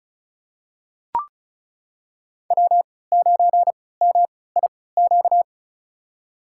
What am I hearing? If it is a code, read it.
Morse code, heard as a keyed tone, about 36 wpm, spaced as for 20 wpm, 700 Hz. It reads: W9MIQ
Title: USA Call Signs - 5 Characters - 3x Character Spacing - 35wpm